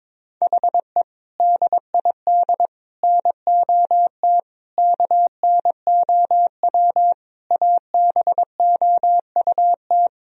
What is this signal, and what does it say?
Morse code, heard as a keyed tone, about 22 wpm, 700 Hz. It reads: HE DID NOT KNOW ABOUT
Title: Sentences from Top 1000 Words - Review Part 3 - 22wpm